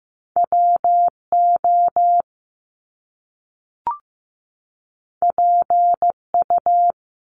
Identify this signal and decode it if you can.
Morse code, heard as a keyed tone, about 15 wpm, 700 Hz. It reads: WO PU
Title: Top Two Letter Combinations in Words 15wpm